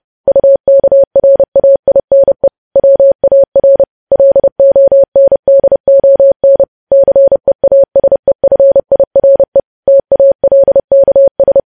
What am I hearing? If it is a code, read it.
Morse code, heard as a keyed tone, about 30 wpm, 550 Hz. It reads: UKRAINE WAR LONDON CEASEFIRE TALKS
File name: podcast / News headlines-1x-30x30